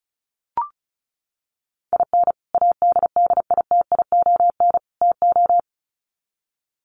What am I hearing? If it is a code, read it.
Morse code, heard as a keyed tone, about 34 wpm, 700 Hz. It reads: IN ADDITION TO
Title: Sets of 3 Words 35wpm